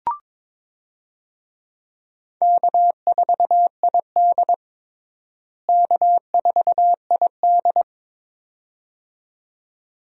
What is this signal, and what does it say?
Morse code, heard as a keyed tone, about 22 wpm, 700 Hz. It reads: K4ID K4ID
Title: USA Call Signs - 4 Characters - Repeated 2x - 22wpm